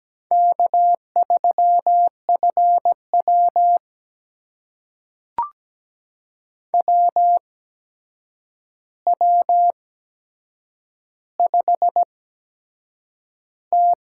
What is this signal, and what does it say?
Morse code, heard as a keyed tone, about 17 wpm, spaced as for 7 wpm, 700 Hz. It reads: K3FW WW5T
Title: USA Call Signs - 4 Characters - 8x Character Spacing - 17wpm